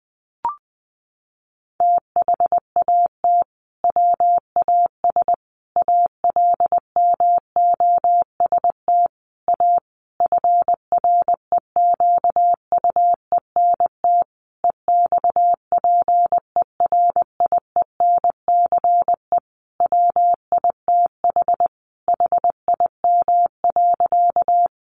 Morse code, heard as a keyed tone, about 20 wpm, 700 Hz. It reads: THAT WAS ALMOST A FREQUENT EXPERIENCE WITH HIM.
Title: Sentences from Top 1400 Words 20wpm